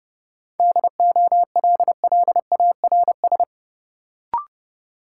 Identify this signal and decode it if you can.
Morse code, heard as a keyed tone, about 30 wpm, 700 Hz. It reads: DOLLARS
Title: Sets of 5 Words - New Word Drill - 30wpm